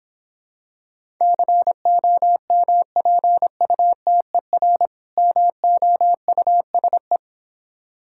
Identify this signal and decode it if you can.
Morse code, heard as a keyed tone, about 26 wpm, 690 Hz. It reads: COMPUTER MOUSE